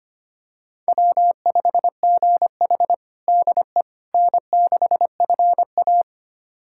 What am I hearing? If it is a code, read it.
Morse code, heard as a keyed tone, about 25 wpm, 700 Hz. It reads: W5GH DE N6FA
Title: USA Callsign DE Callsign - 4 Characters - 25wpm